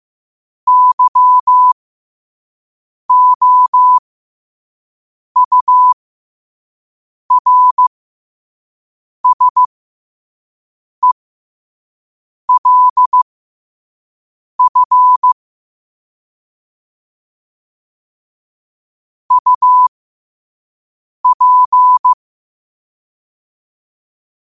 Morse code, heard as a keyed tone, about 15 wpm, spaced as for 5 wpm, 1000 Hz. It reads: YOURSELF UP
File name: qotdfiles / 5wpm